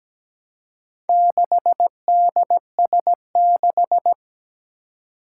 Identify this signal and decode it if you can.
Morse code, heard as a keyed tone, about 17 wpm, 700 Hz. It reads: 6DS6
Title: Four Challenging Letter-Number Combinations 17wpm